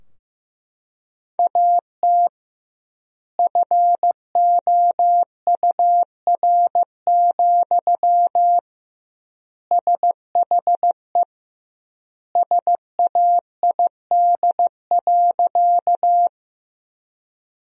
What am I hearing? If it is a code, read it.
Morse code, heard as a keyed tone, about 16 wpm, 700 Hz. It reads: AT FOUR, SHE SAID.
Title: 2x Word-Spacing - Sentences from Top 300 Words - 15wpm